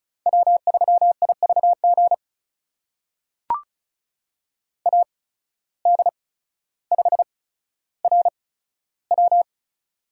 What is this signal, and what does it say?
Morse code, heard as a keyed tone, about 34 wpm, spaced as for 15 wpm, 700 Hz. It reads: W3IVG AD5RW
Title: USA Call Signs - 5 Characters - 8x Character Spacing - 35wpm